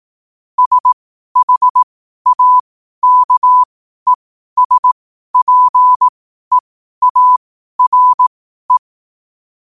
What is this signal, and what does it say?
Morse code, heard as a keyed tone, about 18 wpm, spaced as for 12 wpm, 1000 Hz. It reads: SHAKESPEARE
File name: qotdfiles / 12wpm